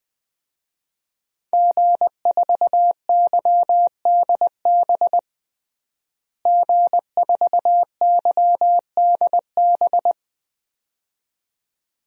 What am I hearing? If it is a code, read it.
Morse code, heard as a keyed tone, about 20 wpm, 700 Hz. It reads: G4YDB G4YDB
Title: DXCC-100 Call Signs - 5 Characters - Repeated 2x - 20wpm